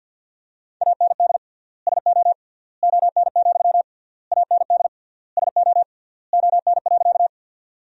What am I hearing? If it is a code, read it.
Morse code, heard as a keyed tone, about 50 wpm, 700 Hz. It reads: AND SO ON, AND SO ON.
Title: 3x Word-Spacing - Sentences from Top 100 Words - 50wpm